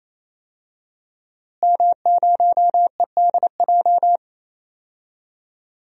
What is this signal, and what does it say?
Morse code, heard as a keyed tone, about 28 wpm, 700 Hz. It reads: M0EDJ
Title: DXCC-100 Call Signs - 5 Characters - Repeated 3x - 28wpm